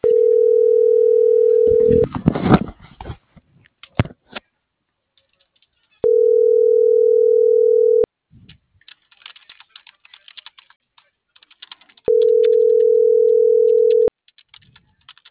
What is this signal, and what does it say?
Ringback tone (North America), 440 + 480 Hz.